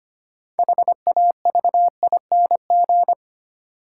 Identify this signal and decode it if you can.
Morse code, heard as a keyed tone, about 25 wpm, 700 Hz. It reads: HAVING
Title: Sets of 5 Words Encore - New Word Drill - 25wpm